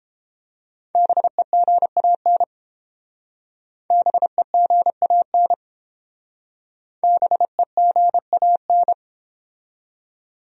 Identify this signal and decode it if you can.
Morse code, heard as a keyed tone, about 30 wpm, 700 Hz. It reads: BEGAN BEGAN BEGAN